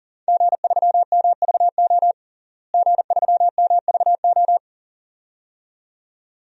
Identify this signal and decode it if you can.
Morse code, heard as a keyed tone, about 37 wpm, 700 Hz. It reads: G3MVO G3MVO
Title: DXCC-100 Call Signs - 5 Characters - Repeated 2x - 40wpm